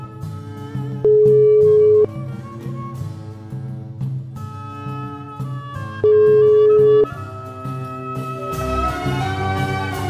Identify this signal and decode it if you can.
Ringback tone (Europe, 425 Hz).